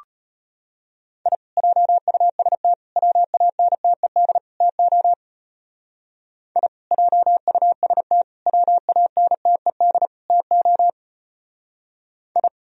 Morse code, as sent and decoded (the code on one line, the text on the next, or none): .. / .--- ..- ... - / .-- .- -. - . -.. / - --- / .. / .--- ..- ... - / .-- .- -. - . -.. / - --- / ..
I JUST WANTED TO I JUST WANTED TO I